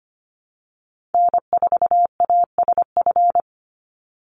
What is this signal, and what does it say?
Morse code, heard as a keyed tone, about 25 wpm, 700 Hz. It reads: N4ASF
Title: Call Signs - 25wpm